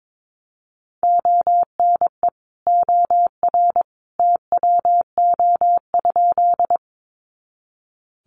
Morse code, heard as a keyed tone, about 22 wpm, 700 Hz. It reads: ONE OR TWO?
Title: Sentences from Top 100 Words 22wpm